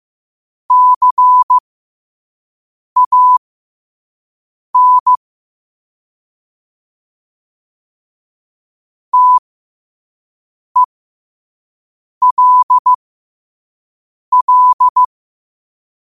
Morse code, heard as a keyed tone, about 15 wpm, spaced as for 5 wpm, 1000 Hz. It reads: CAN TELL
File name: qotdfiles / 5wpm